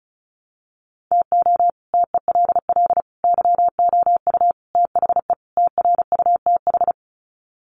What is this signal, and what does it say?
Morse code, heard as a keyed tone, about 34 wpm, 700 Hz. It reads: TO TELL YOU THE TRUTH
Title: Sets of 5 Words 35wpm